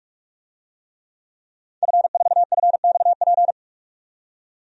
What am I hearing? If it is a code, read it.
Morse code, heard as a keyed tone, about 46 wpm, 700 Hz. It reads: F4FXP
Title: DXCC-100 Call Signs - 5 Characters - 45wpm